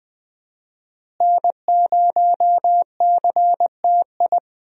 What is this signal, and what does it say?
Morse code, heard as a keyed tone, about 20 wpm, 700 Hz. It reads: N0CTI
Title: USA Call Signs - 5 Characters - Repeated 3x - 20wpm